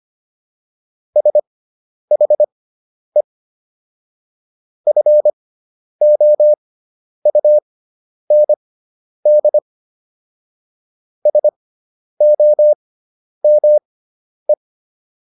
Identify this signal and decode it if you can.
Morse code, heard as a keyed tone, about 25 wpm, spaced as for 10 wpm, 600 Hz. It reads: SHE FOUND SOME